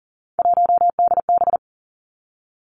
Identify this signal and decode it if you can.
Morse code, heard as a keyed tone, about 42 wpm, 700 Hz. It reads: 1DB
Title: Three Challenging Letter-Number Combinations 40wpm